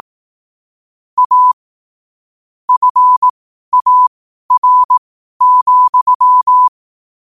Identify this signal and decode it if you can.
Morse code, heard as a keyed tone, about 18 wpm, spaced as for 12 wpm, 1000 Hz. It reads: A FAR,
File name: qotdfiles / 12wpm